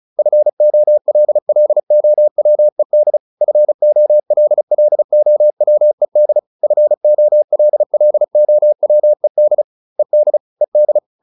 Morse code, heard as a keyed tone, about 34 wpm, 600 Hz. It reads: FOLLOWED FOLLOWED FOLLOWED ED ED